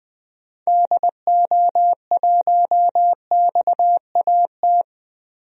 Morse code, heard as a keyed tone, about 20 wpm, 700 Hz. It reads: DO1XAT
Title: DXCC-100 Call Signs - 6 Characters - Repeated 2x - 20wpm